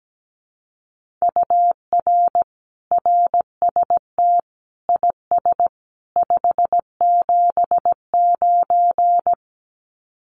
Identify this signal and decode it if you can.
Morse code, heard as a keyed tone, about 17 wpm, 700 Hz. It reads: UR RST IS 579
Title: Common QSO Phrases 17wpm